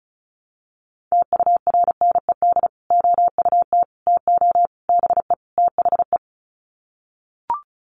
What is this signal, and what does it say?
Morse code, heard as a keyed tone, about 34 wpm, 700 Hz. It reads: TURNED OUT TO BE THE
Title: Sets of 5 Words Encore 35wpm